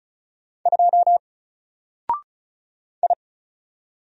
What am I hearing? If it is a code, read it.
Morse code, heard as a keyed tone, about 35 wpm, 700 Hz. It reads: 2 I
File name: Single-Letter-Number-35wpm